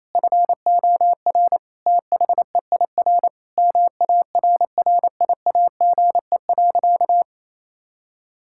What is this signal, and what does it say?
Morse code, heard as a keyed tone, about 28 wpm, 700 Hz. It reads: FOR THEIR MARRIAGE.
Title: Sentences from Top 1000 Words - Review Part 6 - 28wpm